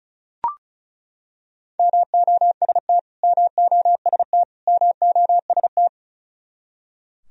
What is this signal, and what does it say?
Morse code, heard as a keyed tone, about 34 wpm, 700 Hz. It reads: MOST MOST MOST
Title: Top 100 Words Repeated 35wpm